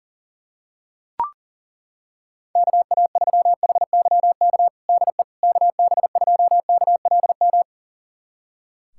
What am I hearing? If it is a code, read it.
Morse code, heard as a keyed tone, about 40 wpm, 700 Hz. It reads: KA3HYK DE KB2KLM
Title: USA Callsign DE Callsign - 6 Characters - 40wpm